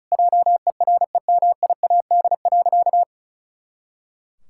Morse code, heard as a keyed tone, about 36 wpm, 700 Hz. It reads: JEREMIAD.
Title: Pangram Challenge v2 - 35wpm